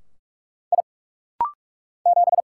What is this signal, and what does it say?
Morse code, heard as a keyed tone, about 44 wpm, 700 Hz. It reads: I 7
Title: Single Letter-Number - Rapid-Fire 45wpm